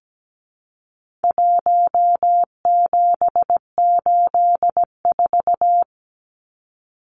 Morse code, heard as a keyed tone, about 17 wpm, 700 Hz. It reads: 1784
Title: Numbers - 4-Digit 17wpm